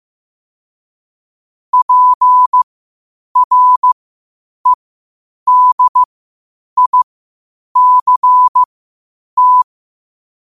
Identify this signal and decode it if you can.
Morse code, heard as a keyed tone, about 15 wpm, spaced as for 8 wpm, 1000 Hz. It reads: PREDICT
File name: qotdfiles / 8wpm